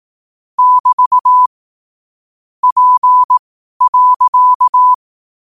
Morse code, heard as a keyed tone, about 18 wpm, spaced as for 12 wpm, 1000 Hz. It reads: <BT> P.